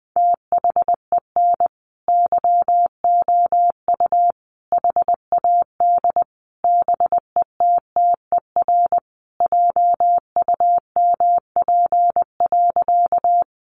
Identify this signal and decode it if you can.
Morse code, heard as a keyed tone, about 20 wpm, 700 Hz. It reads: THEN YOU HAD BETTER JUMP.
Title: Sentences from Top 1300 Words 20wpm